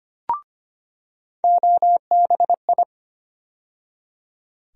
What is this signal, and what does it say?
Morse code, heard as a keyed tone, about 25 wpm, 700 Hz. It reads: OBI
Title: USA Call Sign Postfixes - 3 Letters 25wpm